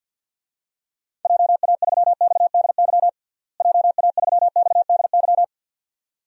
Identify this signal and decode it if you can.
Morse code, heard as a keyed tone, about 53 wpm, 700 Hz. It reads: JA3XDY JA3XDY
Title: DXCC-100 Call Signs - 6 Characters - Repeated 2x - 50wpm